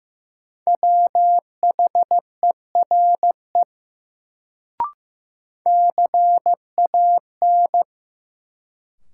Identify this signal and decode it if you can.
Morse code, heard as a keyed tone, about 16 wpm, 700 Hz. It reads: WHERE CAN
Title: Sets of 2 Words - New Word Drill - 15wpm